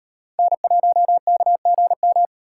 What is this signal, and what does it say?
Morse code, heard as a keyed tone, about 38 wpm, 700 Hz. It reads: N1KGM